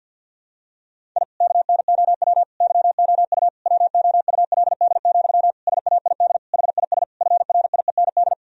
Morse code, heard as a keyed tone, about 49 wpm, 700 Hz. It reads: I KNOW YOU WOULD, SAID HIS FRIEND
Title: Sentences from Top 1000 Words - Review Part 1 - 50wpm